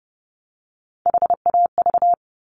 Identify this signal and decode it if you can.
Morse code, heard as a keyed tone, about 30 wpm, 700 Hz. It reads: HAV
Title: Top Three Letter Combinations in Words 30wpm